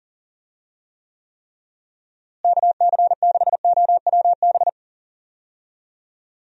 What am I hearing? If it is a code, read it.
Morse code, heard as a keyed tone, about 40 wpm, 700 Hz. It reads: KC6OWB
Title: USA Call Signs - 6 Characters - Repeated 3x - 40wpm